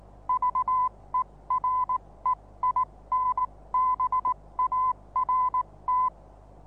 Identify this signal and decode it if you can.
Morse code, heard as a keyed tone, about 19 wpm, spaced as for 16 wpm, 1010 Hz. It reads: VEREINBART
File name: repos / Morsekurs / DJ4UF Morsen26